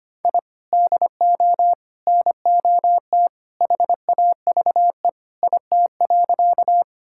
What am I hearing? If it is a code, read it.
Morse code, heard as a keyed tone, about 25 wpm, 700 Hz. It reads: I DO NOT HAVE IT.
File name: practice / 25wpm / Sentences with 3-Word Phrases - 25wpm